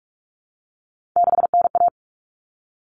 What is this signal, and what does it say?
Morse code, heard as a keyed tone, about 44 wpm, 700 Hz. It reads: 6NA
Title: Common Cut Numbers - 3-Digit 45wpm